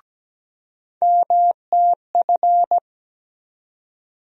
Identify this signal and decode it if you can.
Morse code, heard as a keyed tone, about 17 wpm, 700 Hz. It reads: MTF
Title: USA Call Sign Postfixes - 3 Letters 17wpm